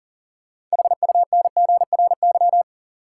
Morse code, heard as a keyed tone, about 40 wpm, 700 Hz. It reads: HUNGRY
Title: Sentences from Top 1400 Words - New Word Drill Alt - 40wpm